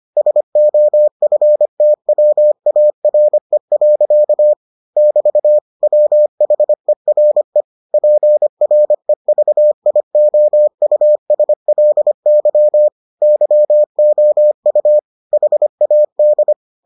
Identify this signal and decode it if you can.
Morse code, heard as a keyed tone, about 25 wpm, 600 Hz. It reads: SOFTWARE. <BT> WHERE PREVIOUSLY YOU HAD